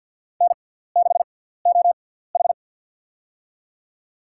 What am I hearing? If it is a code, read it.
Morse code, heard as a keyed tone, about 52 wpm, spaced as for 17 wpm, 690 Hz. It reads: N6OH
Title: USA Call Signs - 4 Characters - 6x Character Spacing - 50wpm